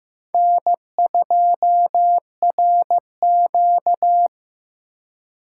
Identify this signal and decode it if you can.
Morse code, heard as a keyed tone, about 18 wpm, 700 Hz. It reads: N2RQ
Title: Speed Racing - USA 1x2 Call Signs 15wpm